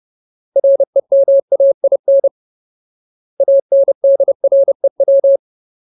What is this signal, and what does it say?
Morse code, heard as a keyed tone, about 30 wpm, 550 Hz. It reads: REMAIN ANDREW
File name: Ultimate Word List / 30WPM Ultimate